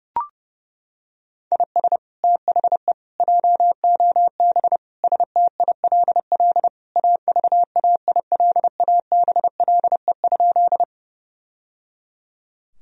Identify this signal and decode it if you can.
Morse code, heard as a keyed tone, about 32 wpm, 700 Hz. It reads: IS THE JOB STILL AVAILABLE?
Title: Sentences from Top 1000 Words - Review Part 3 - 30wpm